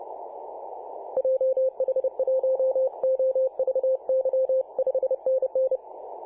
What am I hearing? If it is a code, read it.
Morse code, heard as a keyed tone, about 30 wpm, 520 Hz. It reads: JH1OVY 5NN
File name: c21gc-24-895mhz-cw